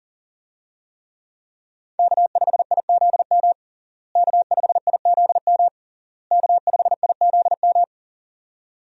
Morse code, heard as a keyed tone, about 40 wpm, 700 Hz. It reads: K5IZM K5IZM K5IZM
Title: USA Call Signs - 5 Characters - Repeated 3x - 40wpm